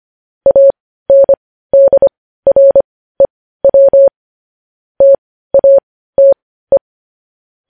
Morse code, heard as a keyed tone, about 25 wpm, spaced as for 15 wpm, 550 Hz. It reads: ANDREW TATE
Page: news-headlines-in-morse-code-at-15-wpm